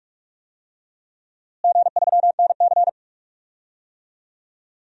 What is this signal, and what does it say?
Morse code, heard as a keyed tone, about 46 wpm, 700 Hz. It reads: G3NC
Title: DXCC-100 Call Signs - 4 Characters - Repeated 3x - 45wpm